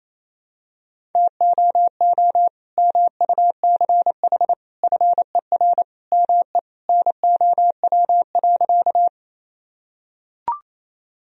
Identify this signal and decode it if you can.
Morse code, heard as a keyed tone, about 27 wpm, 700 Hz. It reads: TOO MUCH FER ME NOW.
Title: Sentences from Top 200 Words and Abbreviations 28wpm